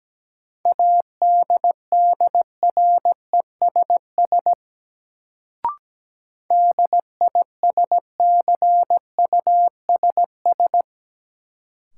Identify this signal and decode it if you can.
Morse code, heard as a keyed tone, about 17 wpm, 700 Hz. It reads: ADDRESS DISCUSS
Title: Sentences from Top 1000 Words - New Word Drill - 17wpm